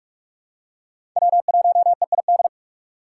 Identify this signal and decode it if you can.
Morse code, heard as a keyed tone, about 46 wpm, 700 Hz. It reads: W1EID
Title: USA Call Signs - 5 Characters - 4x Character Spacing - 45wpm